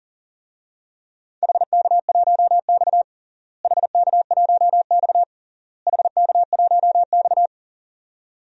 Morse code, heard as a keyed tone, about 40 wpm, 700 Hz. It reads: HK1X HK1X HK1X
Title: DXCC-100 Call Signs - 4 Characters - Repeated 3x - 40wpm